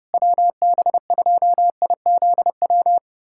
Morse code, heard as a keyed tone, about 30 wpm, 700 Hz. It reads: WB2IZW
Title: USA Callsign DE Callsign 30wpm